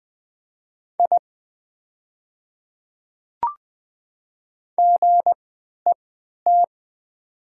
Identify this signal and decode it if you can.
Morse code, heard as a keyed tone, about 20 wpm, spaced as for 11 wpm, 700 Hz. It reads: I GET
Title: 3x Character Spacing - Top 300 Words - 20wpm